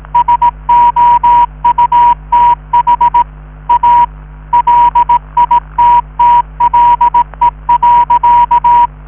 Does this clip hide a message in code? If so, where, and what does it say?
Yes: Morse code from 0.1 to 8.9 s, heard as a keyed tone, about 18 wpm, 960 Hz. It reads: SOUTH A LITTLE.